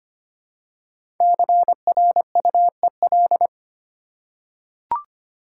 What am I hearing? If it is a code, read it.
Morse code, heard as a keyed tone, about 25 wpm, 700 Hz. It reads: CRUEL
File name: 5 Letter Words - First 3 Letters Given - 25wpm